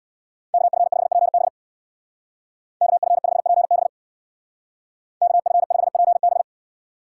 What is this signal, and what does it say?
Morse code, heard as a keyed tone, about 63 wpm, 700 Hz. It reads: KF5PB KF5PB KF5PB